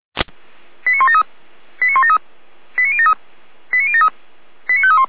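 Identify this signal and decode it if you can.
Five-tone selective call (ZVEI), heard as quick runs of five tones, tones of about 70 ms. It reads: 89173 89173 89073 89073 89531
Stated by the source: ZVEI five-tone selective call received on an SDR.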